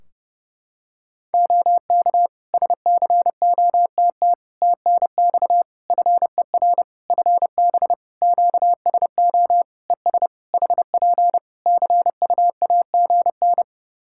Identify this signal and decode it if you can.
Morse code, heard as a keyed tone, about 30 wpm, 700 Hz. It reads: OK SCOTT TNX FER FB QSO ES HP CUAGN
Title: Practice QSO - Exchange 4 - Ending - 30wpm